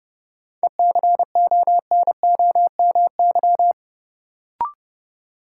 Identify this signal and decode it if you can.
Morse code, heard as a keyed tone, about 31 wpm, 700 Hz. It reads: ECONOMY
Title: Sentences from Top 1200 Words - New Word Drill Alt - 30wpm